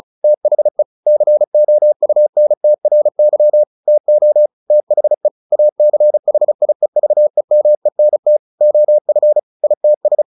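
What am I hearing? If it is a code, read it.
Morse code, heard as a keyed tone, about 34 wpm, 600 Hz. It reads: THE COUNTRY TO THE ACHIEVEMENT OF ITS
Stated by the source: Morse practice at 35 wpm from CW med Gnister, sending Ken Follett's novel Never in English.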